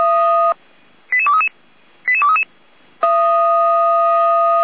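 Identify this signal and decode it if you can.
Five-tone selective call (ZVEI), heard as quick runs of five tones, tones of about 70 ms. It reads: 88233 88233
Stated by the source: ZVEI five-tone call 88192 received by SDR, fire brigade Feuerwehr Orlamünde (f85360).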